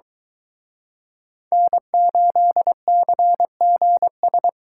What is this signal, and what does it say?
Morse code, heard as a keyed tone, about 23 wpm, 700 Hz. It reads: N8CGS